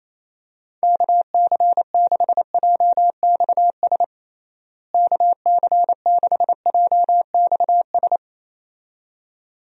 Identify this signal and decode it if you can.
Morse code, heard as a keyed tone, about 28 wpm, 700 Hz. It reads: KC6JXS KC6JXS
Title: USA Call Signs - 6 Characters - Repeated 2x - 28wpm